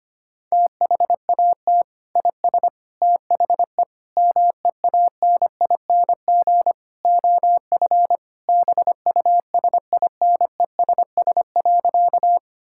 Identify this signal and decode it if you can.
Morse code, heard as a keyed tone, about 25 wpm, 700 Hz. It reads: THAT IS THE MEANING OF BUSINESS.
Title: Sentences from Top 1000 Words - Review Part 6 - 25wpm